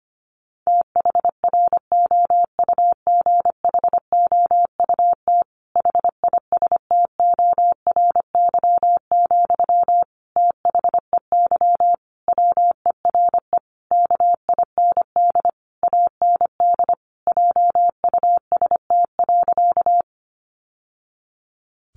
Morse code, heard as a keyed tone, about 25 wpm, 700 Hz. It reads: THROUGHOUT HISTORY, THEY WERE KIND AND JUST.
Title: Sentences from Top 800 Words 25wpm